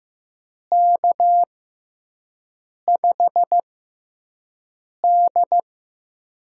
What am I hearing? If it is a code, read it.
Morse code, heard as a keyed tone, about 15 wpm, spaced as for 5 wpm, 700 Hz. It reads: K5D
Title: USA Call Signs - 4 Characters - 6x Character Spacing - 15wpm